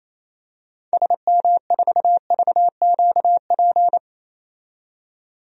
Morse code, heard as a keyed tone, about 27 wpm, 700 Hz. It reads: SM4VQP
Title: DXCC-100 Call Signs - 6 Characters - 28wpm